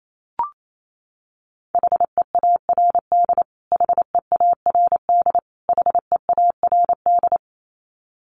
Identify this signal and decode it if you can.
Morse code, heard as a keyed tone, about 28 wpm, 700 Hz. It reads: HEARD HEARD HEARD
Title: Top 500 Words Repeated 28wpm